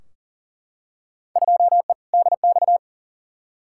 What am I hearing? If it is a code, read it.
Morse code, heard as a keyed tone, about 40 wpm, 700 Hz. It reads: 2E DX
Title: Field Day Exchanges - 40wpm